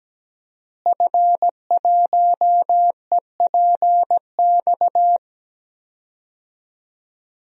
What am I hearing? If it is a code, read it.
Morse code, heard as a keyed tone, about 17 wpm, 700 Hz. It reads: F1EPX
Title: DXCC-100 Call Signs - 5 Characters - Repeated 2x - 17wpm